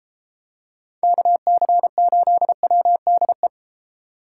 Morse code, heard as a keyed tone, about 31 wpm, 700 Hz. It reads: KC8WDE